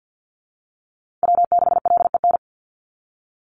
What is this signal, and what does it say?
Morse code, heard as a keyed tone, about 50 wpm, 700 Hz. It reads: W6LEN
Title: Call Signs - 50wpm